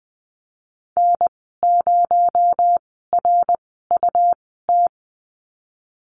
Morse code, heard as a keyed tone, about 20 wpm, 700 Hz. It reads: N 0 R U T
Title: 238 Lesson 38g - Letter q - Call Signs 20wpm